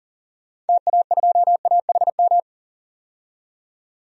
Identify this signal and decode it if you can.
Morse code, heard as a keyed tone, about 40 wpm, 700 Hz. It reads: TA2AHM
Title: DXCC-100 Call Signs - 6 Characters - 40wpm